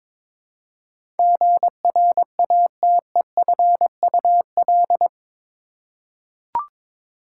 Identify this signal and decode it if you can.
Morse code, heard as a keyed tone, about 22 wpm, 700 Hz. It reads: GRATEFUL
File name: practice / 22wpm / Eight Letter Words - 22wpm